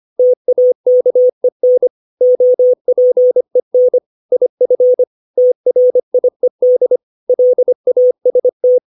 Morse code, heard as a keyed tone, about 25 wpm, 500 Hz. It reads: TAKEN OPEN IF TRIED LAST
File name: 632 words / 632 Words 25WPM